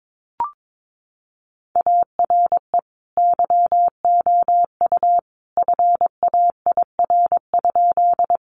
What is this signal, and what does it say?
Morse code, heard as a keyed tone, about 22 wpm, 700 Hz. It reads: ARE YOU FAIR?
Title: Sentences from Top 1300 Words 22wpm